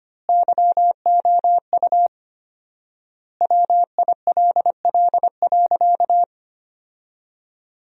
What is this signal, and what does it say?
Morse code, heard as a keyed tone, about 25 wpm, 700 Hz. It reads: YOU WILL.
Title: 4x Word-Spacing - Sentences from Top 500 Words - 25wpm